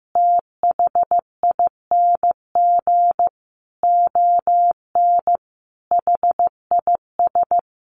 Morse code, heard as a keyed tone, about 15 wpm, 700 Hz. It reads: THING ON HIS
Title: Sentences from Top 1100 Words 15wpm